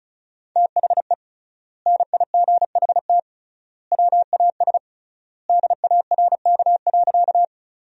Morse code, heard as a keyed tone, about 34 wpm, 700 Hz. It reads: THE NIGHT WAS DARK.
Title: 3x Word-Spacing - Sentences from Top 500 Words - 35wpm